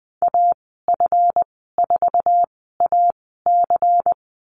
Morse code, heard as a keyed tone, about 20 wpm, spaced as for 14 wpm, 700 Hz. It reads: AF4AC
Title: 161 Lesson 26g - Letter g - Call Signs 20wpm